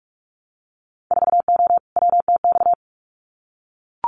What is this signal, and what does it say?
Morse code, heard as a keyed tone, about 44 wpm, 700 Hz. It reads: 4O WTX
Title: Winter Field Day Exchanges Encore 45wpm